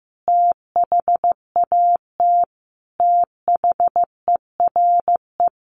Morse code, heard as a keyed tone, about 15 wpm, 700 Hz. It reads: THAT THERE
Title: Sentences from Top 1300 Words 15wpm